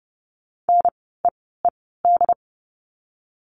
Morse code, heard as a keyed tone, about 30 wpm, spaced as for 17 wpm, 700 Hz. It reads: NEED
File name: Lesson 17f - Letter u - Words 30wpm